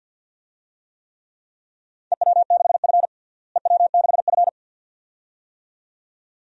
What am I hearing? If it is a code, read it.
Morse code, heard as a keyed tone, about 51 wpm, 700 Hz. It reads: EW6F EW6F
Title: DXCC-100 Call Signs - 4 Characters - Repeated 2x - 50wpm